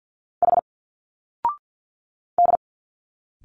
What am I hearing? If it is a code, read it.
Morse code, heard as a keyed tone, about 50 wpm, 700 Hz. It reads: H D
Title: Single Letters 50wpm